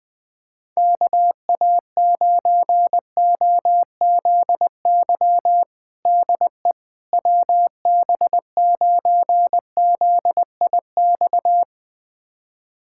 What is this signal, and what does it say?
Morse code, heard as a keyed tone, about 20 wpm, 700 Hz. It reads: KA9OZY DE WB9ZIX
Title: USA Callsign DE Callsign - 6 Characters - 20wpm